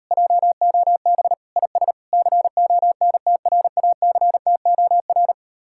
Morse code, heard as a keyed tone, about 39 wpm, 700 Hz. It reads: JOB IS CONTRACTOR